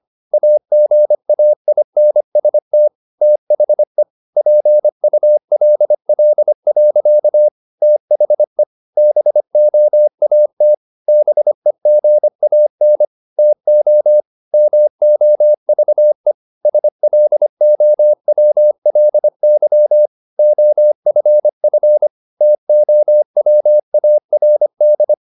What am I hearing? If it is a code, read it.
Morse code, heard as a keyed tone, about 25 wpm, 600 Hz. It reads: AGAINST THE PULL. THE BOAT BEGAN TO MOVE SLOWLY OFF TOWARD
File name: the old man and the sea part 1 25wpm